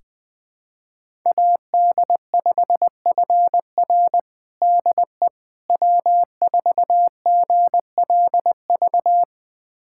Morse code, heard as a keyed tone, about 21 wpm, 700 Hz. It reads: AD5FR DE W4GLV
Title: USA Callsign DE Callsign - 5 Characters - 20wpm